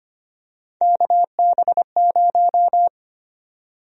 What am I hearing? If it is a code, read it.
Morse code, heard as a keyed tone, about 25 wpm, 700 Hz. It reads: KB0
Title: USA Call Sign Prefixes - 25wpm